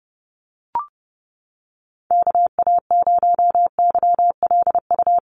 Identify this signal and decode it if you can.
Morse code, heard as a keyed tone, about 30 wpm, 700 Hz. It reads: KA0YLU
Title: USA Call Signs Repeated 2x 30wpm